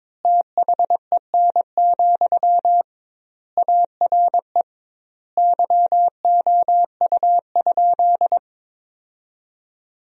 Morse code, heard as a keyed tone, about 22 wpm, 700 Hz. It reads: THEN, ARE YOU?
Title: 2x Word-Spacing - Sentences from Top 100 Words - 22wpm